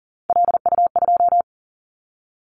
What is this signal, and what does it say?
Morse code, heard as a keyed tone, about 41 wpm, 700 Hz. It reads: LU2
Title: Callsign Prefixes for Top 50 DXCC - 40wpm